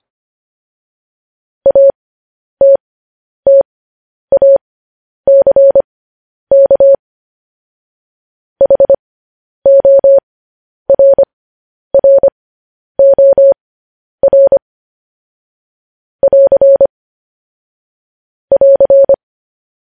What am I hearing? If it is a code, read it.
Morse code, heard as a keyed tone, about 25 wpm, spaced as for 10 wpm, 550 Hz. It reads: ATTACK HORROR <AR> <AR>
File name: podcast / News headlines-1x-25x10